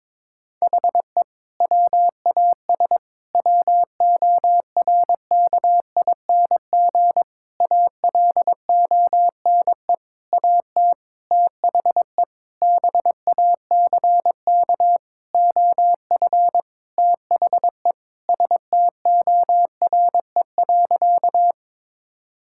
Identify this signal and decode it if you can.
Morse code, heard as a keyed tone, about 22 wpm, 700 Hz. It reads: HE WAS WORKING ALONE AT THE BACK OF THE STORE.
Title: Sentences from Top 1000 Words - Review Part 5 - 22wpm